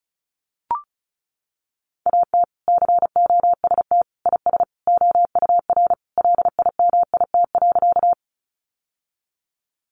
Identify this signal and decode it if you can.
Morse code, heard as a keyed tone, about 34 wpm, 700 Hz. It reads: AT COST IS OUR LIMIT.
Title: Sentences from Top 1500 Words 35wpm